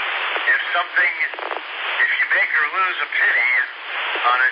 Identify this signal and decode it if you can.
Emergency Alert System (SAME) tones, heard as harsh buzzing bursts, with speech. It is heard from 0.9 to 3.7 s.